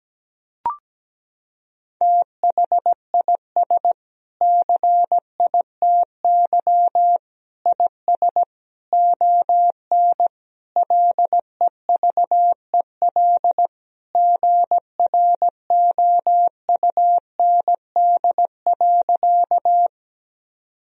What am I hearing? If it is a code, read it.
Morse code, heard as a keyed tone, about 17 wpm, 700 Hz. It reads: THIS CITY IS ON LEVEL GROUND.
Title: Sentences from Top 1000 Words - Review Part 4 - 17wpm